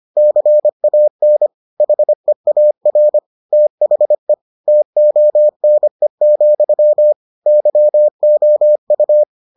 Morse code, heard as a keyed tone, about 25 wpm, 600 Hz. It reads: CAN HEAR THE TONE, YOU